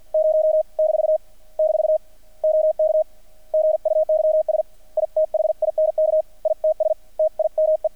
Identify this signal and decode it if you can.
Morse code, heard as a keyed tone, about 50 wpm, 630 Hz. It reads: 0 - - OK GUYS I THINK ITS TIME
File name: archived / CRSnet